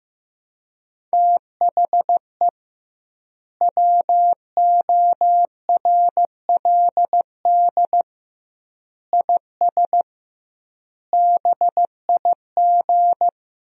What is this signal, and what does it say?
Morse code, heard as a keyed tone, about 15 wpm, 700 Hz. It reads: THE WORLD IS BIG
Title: 2x Word-Spacing - Sentences from Top 400 Words - 15wpm